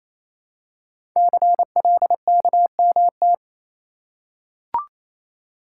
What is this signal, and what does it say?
Morse code, heard as a keyed tone, about 28 wpm, 700 Hz. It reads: CLKMT
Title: Code Groups - 5 Letters - 28wpm